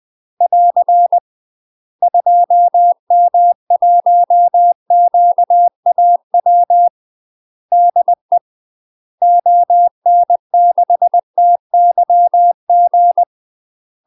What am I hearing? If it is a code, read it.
Morse code, heard as a keyed tone, about 20 wpm, 700 Hz. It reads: <AR> 2M1QAW DE ON6TYG
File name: UK MASTER QSOs TRAINING / 20WPM / UK MASTER QSOs 0070